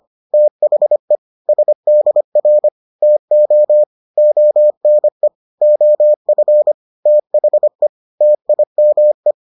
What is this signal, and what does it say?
Morse code, heard as a keyed tone, about 25 wpm, 600 Hz. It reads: THE SDR TO ONE OF THE TIME